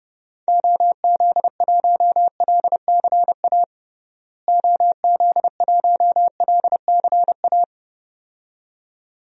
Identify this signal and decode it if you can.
Morse code, heard as a keyed tone, about 32 wpm, 700 Hz. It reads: OZ1LCA OZ1LCA
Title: DXCC-100 Call Signs - 6 Characters - Repeated 2x - 30wpm